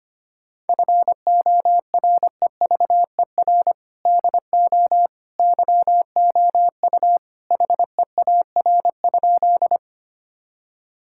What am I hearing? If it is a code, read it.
Morse code, heard as a keyed tone, about 25 wpm, 700 Hz. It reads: FOREVER DO YOU HEAR?
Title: Sentences from Top 1600 Words 25wpm